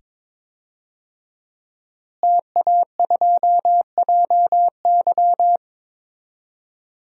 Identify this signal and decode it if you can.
Morse code, heard as a keyed tone, about 22 wpm, 700 Hz. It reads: TA2JY
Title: DXCC-100 Call Signs - 5 Characters - Repeated 2x - 22wpm